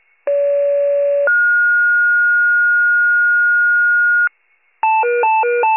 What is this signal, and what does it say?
Two-tone page, heard as a one-second tone, then a three-second tone, about 570 then 1430 Hz. It is heard from 0.3 to 4.3 s.